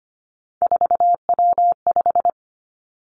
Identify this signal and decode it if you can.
Morse code, heard as a keyed tone, about 25 wpm, 700 Hz. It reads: 4W5
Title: Three Challenging Letter-Number Combinations 25wpm